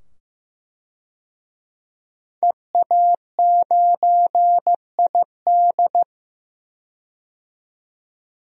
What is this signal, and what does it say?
Morse code, heard as a keyed tone, about 15 wpm, 700 Hz. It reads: EA9ID
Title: DXCC-100 Call Signs - 5 Characters - 15wpm